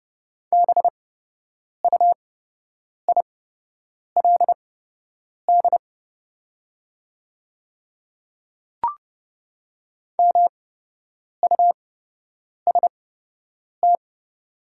Morse code, heard as a keyed tone, about 30 wpm, spaced as for 8 wpm, 700 Hz. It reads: BUILD MUST
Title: 8x Character Spacing - Top 300 Words - 30wpm